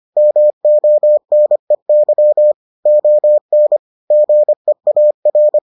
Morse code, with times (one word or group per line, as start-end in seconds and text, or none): 0.2-2.5 MONEY
2.9-3.8 ON
4.1-5.6 GEAR